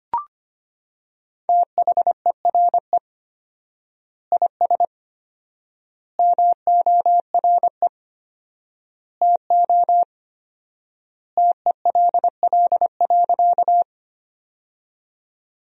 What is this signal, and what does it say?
Morse code, heard as a keyed tone, about 25 wpm, 700 Hz. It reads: THERE IS MORE TO TELL.
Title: 4x Word-Spacing - Sentences from Top 200 Words - 25wpm